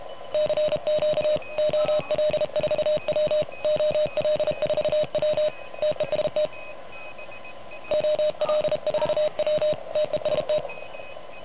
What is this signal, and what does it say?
Morse code, heard as a keyed tone, about 32 wpm, 590 Hz. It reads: CQ OL4W OL4W TEST OL4W TEST
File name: truSDX / MWC29.8